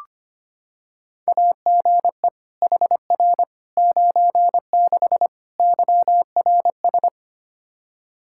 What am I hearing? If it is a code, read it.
Morse code, heard as a keyed tone, about 25 wpm, 700 Hz. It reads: AGE HR 96 YRS
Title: Practice QSO - Exchange 3 - More Info - 25wpm